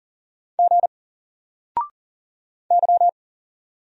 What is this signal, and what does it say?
Morse code, heard as a keyed tone, about 40 wpm, 700 Hz. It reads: G Y
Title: Single Character (Letter:Number:Punctuation:Prosign) Alt 40wpm